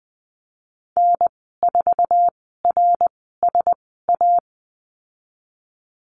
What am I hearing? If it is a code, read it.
Morse code, heard as a keyed tone, about 20 wpm, spaced as for 15 wpm, 700 Hz. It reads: N4RSA
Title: 076 Lesson 13g - Letter L - Call Signs 20wpm